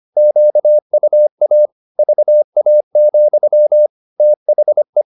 Morse code, heard as a keyed tone, about 25 wpm, 600 Hz. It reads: QUA VA, THE
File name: the old man and the sea part 1 25wpm